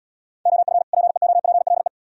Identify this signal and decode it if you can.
Morse code, heard as a keyed tone, about 76 wpm, 700 Hz. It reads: OF PEOPLE